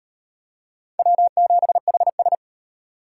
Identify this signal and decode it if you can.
Morse code, heard as a keyed tone, about 38 wpm, 700 Hz. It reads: W7HS